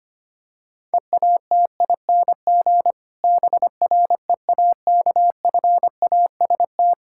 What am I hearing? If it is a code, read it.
Morse code, heard as a keyed tone, about 25 wpm, 700 Hz. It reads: EATING BREAKFAST
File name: practice / 25wpm / Difficult Sentences with Hints - 25wpm